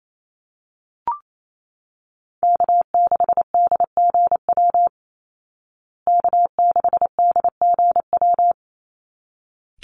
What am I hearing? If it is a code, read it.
Morse code, heard as a keyed tone, about 28 wpm, 700 Hz. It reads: K6DGW K6DGW
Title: Call Signs Repeated 3x - 28wpm